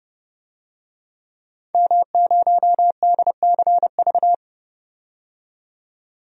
Morse code, heard as a keyed tone, about 30 wpm, 690 Hz. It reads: M0DCV
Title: DXCC-100 Call Signs - 5 Characters - 30wpm